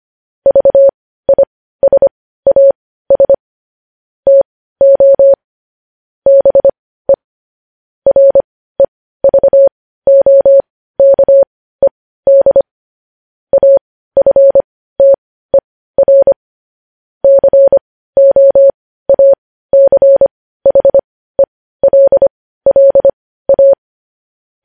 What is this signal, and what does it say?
Morse code, heard as a keyed tone, about 25 wpm, spaced as for 15 wpm, 550 Hz. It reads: VISAS TO BE REVOKED AFTER COACHELLA
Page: news headlines in morse code at 15 wpm